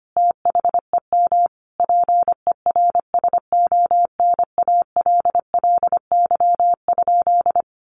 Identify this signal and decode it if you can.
Morse code, heard as a keyed tone, about 25 wpm, 700 Hz. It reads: THEM PERSONALLY?
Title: Sentences from Top 1400 Words 25wpm